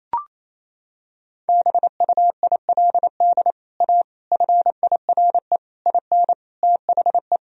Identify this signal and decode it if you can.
Morse code, heard as a keyed tone, about 28 wpm, 700 Hz. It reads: BUILD A FIRE IN THE
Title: Sentences from Top 1000 Words - Review Part 4 - 28wpm